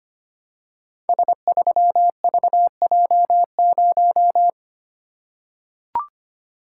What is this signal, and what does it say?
Morse code, heard as a keyed tone, about 25 wpm, 700 Hz. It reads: S3VJ0
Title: Code Groups - 5 Letter:Numbers - 25wpm